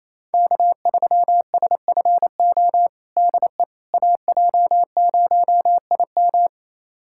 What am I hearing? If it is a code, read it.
Morse code, heard as a keyed tone, about 28 wpm, 700 Hz. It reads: K3SFO DE AJ0IM
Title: USA Callsign DE Callsign - 5 Characters - 28wpm